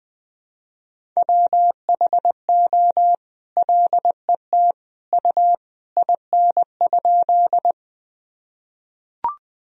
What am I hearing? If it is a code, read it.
Morse code, heard as a keyed tone, about 20 wpm, 700 Hz. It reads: WHO LET U IN?
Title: Sentences from Top 300 Words and Abbreviations 20wpm